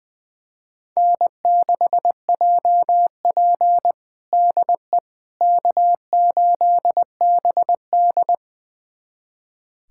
Morse code, heard as a keyed tone, about 20 wpm, 700 Hz. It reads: N6JP DE K8BD
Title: USA Callsign DE Callsign - 4 Characters - 20wpm